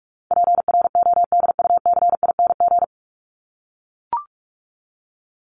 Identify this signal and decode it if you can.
Morse code, heard as a keyed tone, about 44 wpm, 700 Hz. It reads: PRODUCING
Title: ING Suffix - Top 100 Words with Base Words 45wpm